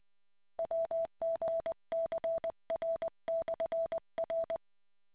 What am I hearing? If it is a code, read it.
Morse code, heard as a keyed tone, about 20 wpm, 660 Hz. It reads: WGCR/R